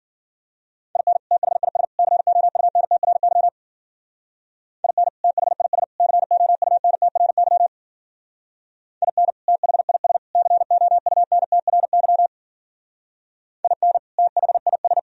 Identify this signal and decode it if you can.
Morse code, heard as a keyed tone, about 48 wpm, 700 Hz. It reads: IN THIS COUNTRY IN THIS COUNTRY IN THIS COUNTRY IN THIS
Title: Speed-Racing - Sets of 3 Words 40wpm